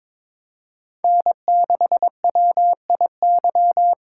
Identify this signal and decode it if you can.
Morse code, heard as a keyed tone, about 22 wpm, 700 Hz. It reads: N6WIY
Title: USA Call Signs - 4x Character Spacing - 22wpm